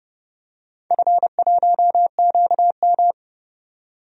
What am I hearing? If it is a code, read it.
Morse code, heard as a keyed tone, about 30 wpm, 700 Hz. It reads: F1QM
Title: DXCC-100 Call Signs - 4 Characters - 30wpm